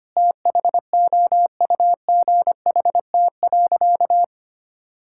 Morse code, heard as a keyed tone, about 25 wpm, 700 Hz. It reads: THOUGHT.
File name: practice / 25wpm / Difficult Sentences with Hints - 25wpm